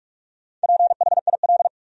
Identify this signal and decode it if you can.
Morse code, heard as a keyed tone, about 46 wpm, 700 Hz. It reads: PHIL
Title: USA Common Names - Max 5 Letters - 45wpm